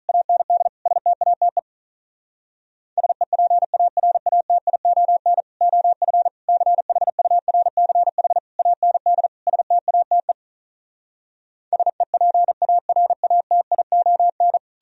Morse code, heard as a keyed tone, about 46 wpm, 700 Hz. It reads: AND STATE SEPARATION OF CHURCH AND STATE SEPARATION